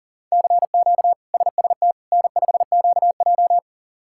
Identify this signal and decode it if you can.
Morse code, heard as a keyed tone, about 40 wpm, 690 Hz. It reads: CQ SST N5QJ
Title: K1USN SST QSO Practice - 40wpm